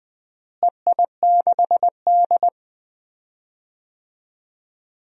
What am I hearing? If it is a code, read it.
Morse code, heard as a keyed tone, about 21 wpm, 700 Hz. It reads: EI6D
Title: DXCC-100 Call Signs - 4 Characters - Repeated 2x - 20wpm